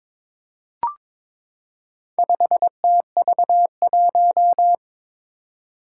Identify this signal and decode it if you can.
Morse code, heard as a keyed tone, about 21 wpm, 700 Hz. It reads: 5TV1
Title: Four Challenging Letter-Number Combinations 22wpm